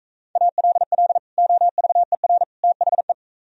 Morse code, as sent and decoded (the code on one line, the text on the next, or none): .- .-.. .-.. / --- ...- . .-. / - .... .
ALL OVER THE